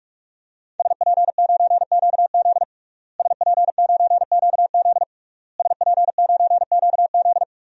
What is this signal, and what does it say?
Morse code, heard as a keyed tone, about 42 wpm, 700 Hz. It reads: SP9QZ SP9QZ SP9QZ
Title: DXCC-100 Call Signs - 5 Characters - Repeated 3x - 45wpm